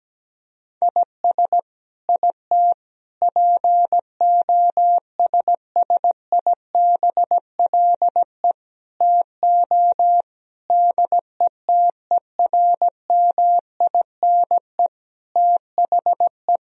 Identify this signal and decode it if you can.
Morse code, heard as a keyed tone, about 17 wpm, 700 Hz. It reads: IS IT POSSIBLE TO DETERMINE THE
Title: Sentences from Top 1000 Words - Review Part 4 - 17wpm